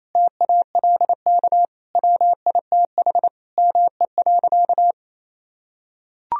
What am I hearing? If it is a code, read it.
Morse code, heard as a keyed tone, about 28 wpm, 700 Hz. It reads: TALK WITH ME.
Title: Sentences from Top 1000 Words - Review Part 3 - 28wpm